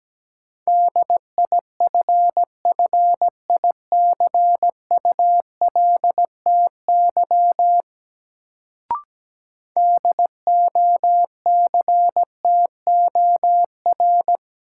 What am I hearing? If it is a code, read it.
Morse code, heard as a keyed tone, about 17 wpm, 700 Hz. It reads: DIFFICULTY DOCTOR
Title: Sentences from Top 1500 Words - New Word Drill Alt - 17wpm